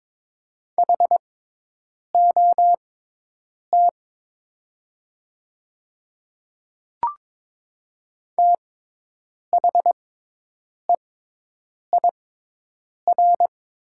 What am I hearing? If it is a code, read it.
Morse code, heard as a keyed tone, about 22 wpm, spaced as for 8 wpm, 700 Hz. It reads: HOT THEIR
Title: 6x Character Spacing - Top 200 Words - 22wpm